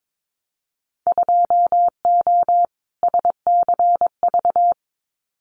Morse code, heard as a keyed tone, about 22 wpm, 700 Hz. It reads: 2O SCV
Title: Winter Field Day Exchanges Encore 22wpm